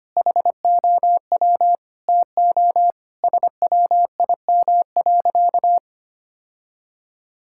Morse code, heard as a keyed tone, about 25 wpm, 700 Hz. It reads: HOW TO SWIM.
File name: practice / 25wpm / Sentences with 3-Word Phrases - 25wpm